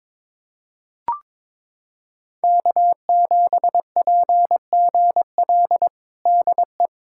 Morse code, heard as a keyed tone, about 22 wpm, 700 Hz. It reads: K7PGL DE
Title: USA Callsign DE Callsign - 5 Characters - 22wpm